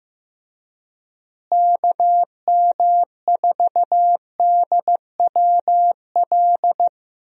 Morse code, heard as a keyed tone, about 15 wpm, 700 Hz. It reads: KM4DWL
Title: USA Call Signs - 6 Characters - Repeated 2x - 15wpm